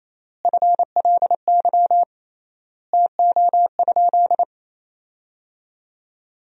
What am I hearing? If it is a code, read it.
Morse code, heard as a keyed tone, about 28 wpm, 700 Hz. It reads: FLY TO?
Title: 3x Word-Spacing - Sentences from Top 500 Words - 28wpm